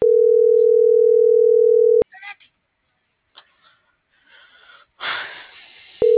Ringback tone (North America), 440 + 480 Hz.